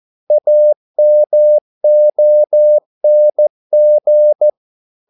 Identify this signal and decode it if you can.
Morse code, heard as a keyed tone, about 14 wpm, 600 Hz. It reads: AMONG
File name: mp3 files / a dogs mistake - 14wpm